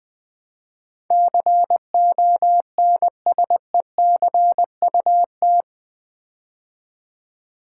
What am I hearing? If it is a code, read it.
Morse code, heard as a keyed tone, about 20 wpm, 700 Hz. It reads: CONSECUT
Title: Build A Word - 4 Letters At A Time - 11-Letter Words - 20wpm